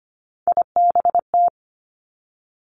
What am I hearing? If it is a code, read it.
Morse code, heard as a keyed tone, about 25 wpm, 700 Hz. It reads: IBT
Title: Three Challenging Letter-Number Combinations 25wpm